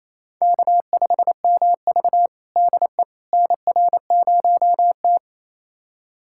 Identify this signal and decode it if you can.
Morse code, heard as a keyed tone, about 28 wpm, 700 Hz. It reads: K5MV DE NR0T
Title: USA Callsign DE Callsign - 4 Characters - 28wpm